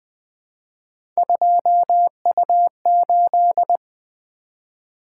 Morse code, heard as a keyed tone, about 20 wpm, 700 Hz. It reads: 2U8
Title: Three Letter-Number Combinations - 20wpm